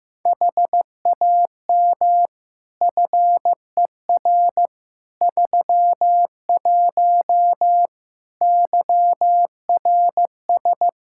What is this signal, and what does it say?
Morse code, heard as a keyed tone, about 15 wpm, 700 Hz. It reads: HAM FER 31 YRS
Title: Practice QSO - Exchange 3 - More Info - 15wpm